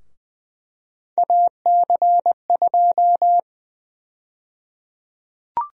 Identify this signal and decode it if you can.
Morse code, heard as a keyed tone, about 21 wpm, 700 Hz. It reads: AC2
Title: USA Call Sign Prefixes - Repeated 2x - 20wpm